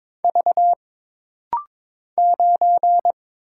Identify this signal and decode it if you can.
Morse code, heard as a keyed tone, about 22 wpm, 700 Hz. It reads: V 9
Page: This rapid-fire timing is geared to push your ICR (Instant Character Recognition) skills.